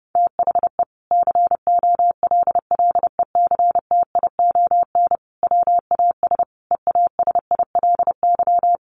Morse code, heard as a keyed tone, about 30 wpm, 700 Hz. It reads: THE COLLECTION WAS EASILY
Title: Sentences from Top 1200 Words 30wpm